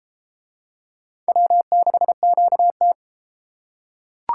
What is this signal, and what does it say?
Morse code, heard as a keyed tone, about 32 wpm, 700 Hz. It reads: W6QT